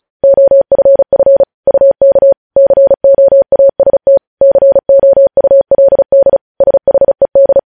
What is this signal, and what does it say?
Morse code, heard as a keyed tone, about 36 wpm, 550 Hz. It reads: OFF UK COAST COULD SHED
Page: news headlines in morse code at 35 wpm